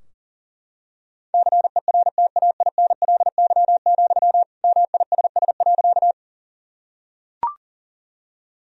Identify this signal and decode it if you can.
Morse code, heard as a keyed tone, about 40 wpm, 700 Hz. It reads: CERTAINLY, MISS.
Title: Sentences from Top 1000 Words - Review Part 3 - 40wpm